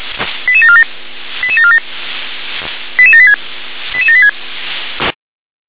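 Five-tone selective call (ZVEI), heard as quick runs of five tones, tones of about 70 ms. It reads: 88747 88747 88786 88786
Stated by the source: ZVEI five-tone selective call 88069, received by SDR.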